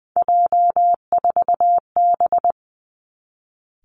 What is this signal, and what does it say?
Morse code, heard as a keyed tone, about 20 wpm, 700 Hz. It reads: J4B
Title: Three Challenging Letter-Number Combinations 20wpm